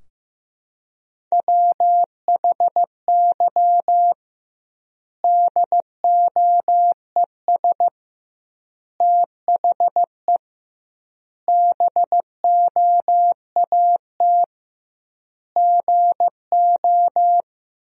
Morse code, heard as a keyed tone, about 15 wpm, 700 Hz. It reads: WHY DOES THE BOAT GO
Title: 2x Word-Spacing - Sentences from Top 500 Words - 15wpm